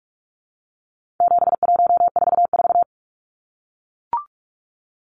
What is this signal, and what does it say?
Morse code, heard as a keyed tone, about 44 wpm, 700 Hz. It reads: 7144
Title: Numbers - 4-Digit 45wpm